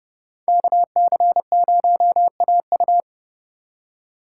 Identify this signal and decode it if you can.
Morse code, heard as a keyed tone, about 31 wpm, 700 Hz. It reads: KC0AU
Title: Speed Racing - USA 2x2 Call Signs 25wpm